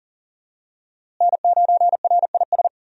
Morse code, heard as a keyed tone, about 37 wpm, 700 Hz. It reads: N9RIS
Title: USA Call Signs - 5 Characters - Repeated 2x - 40wpm